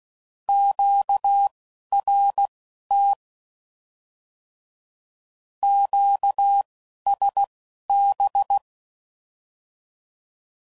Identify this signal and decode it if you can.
Morse code, heard as a keyed tone, about 16 wpm, spaced as for 11 wpm, 800 Hz. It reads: QRT QSB